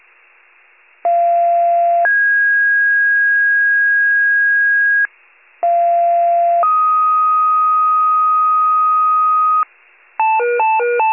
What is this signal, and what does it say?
Two-tone page, heard as a one-second tone, then a three-second tone, about 690 then 1650 Hz, about 690 then 1250 Hz, with noise. It is heard from 1.0 to 9.6 s.
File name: wcalert